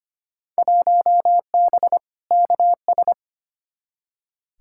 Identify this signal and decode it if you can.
Morse code, heard as a keyed tone, about 25 wpm, 700 Hz. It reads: 1B KS
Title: Field Day Exchanges Encore - 25wpm